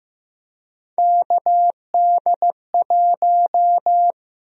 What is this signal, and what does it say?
Morse code, heard as a keyed tone, about 15 wpm, 700 Hz. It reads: KD1
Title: USA Call Sign Prefixes - Repeated 2x - 15wpm